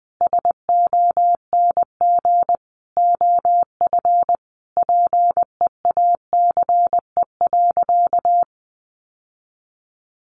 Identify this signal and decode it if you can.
Morse code, heard as a keyed tone, about 20 wpm, 700 Hz. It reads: SONG OF PEACE.
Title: Sentences from Top 1500 Words 20wpm